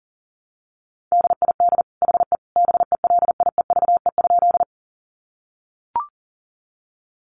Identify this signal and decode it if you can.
Morse code, heard as a keyed tone, about 40 wpm, 700 Hz. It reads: DID HE BELIEVE?
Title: Sentences from Top 1000 Words 40wpm